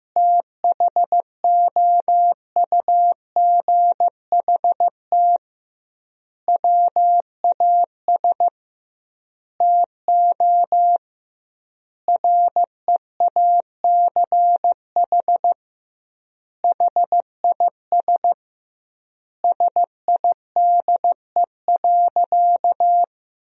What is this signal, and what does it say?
Morse code, heard as a keyed tone, about 15 wpm, 700 Hz. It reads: THOUGHT WAS TO REACH HIS SIDE.
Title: 2x Word-Spacing - Sentences from Top 400 Words - 15wpm